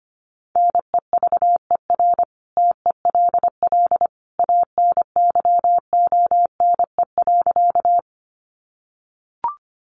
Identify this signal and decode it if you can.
Morse code, heard as a keyed tone, about 25 wpm, 700 Hz. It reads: NEVER TELL ANYONE.
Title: Sentences from Top 900 Words 25wpm